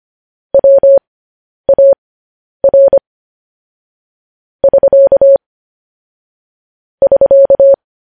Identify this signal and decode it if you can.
Morse code, heard as a keyed tone, about 25 wpm, spaced as for 10 wpm, 550 Hz. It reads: WAR <SK> <SK>
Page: news headlines in morse code at 10 wpm